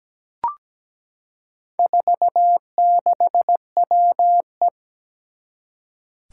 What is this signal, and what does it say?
Morse code, heard as a keyed tone, about 17 wpm, 700 Hz. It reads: 46WE
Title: Four Challenging Letter-Number Combinations 17wpm